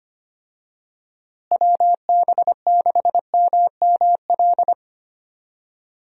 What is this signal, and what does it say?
Morse code, heard as a keyed tone, about 25 wpm, 700 Hz. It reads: WB6MML
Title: USA Call Signs - 6 Characters - Repeated 3x - 25wpm